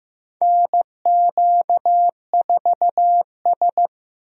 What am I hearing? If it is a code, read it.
Morse code, heard as a keyed tone, about 15 wpm, 700 Hz. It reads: NQ4S
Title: USA Call Signs - 4 Characters - Repeated 2x - 15wpm